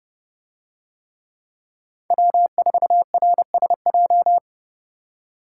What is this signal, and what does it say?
Morse code, heard as a keyed tone, about 31 wpm, 700 Hz. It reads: W4RSJ